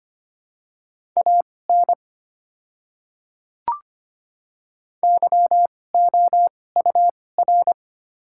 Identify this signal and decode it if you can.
Morse code, heard as a keyed tone, about 25 wpm, spaced as for 18 wpm, 700 Hz. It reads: AN YOUR
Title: 2x Character Spacing - Top 300 Words - 25wpm